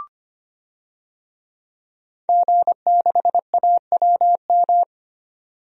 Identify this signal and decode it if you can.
Morse code, heard as a keyed tone, about 25 wpm, 700 Hz. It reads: G6AWM
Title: DXCC-100 Call Signs - 5 Characters - 25wpm